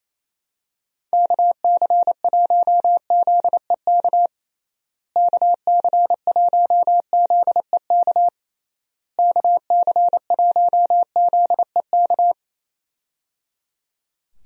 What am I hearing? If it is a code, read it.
Morse code, heard as a keyed tone, about 28 wpm, 700 Hz. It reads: KC1ZEK KC1ZEK KC1ZEK
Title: USA Call Signs - 6 Characters - Repeated 3x - 28wpm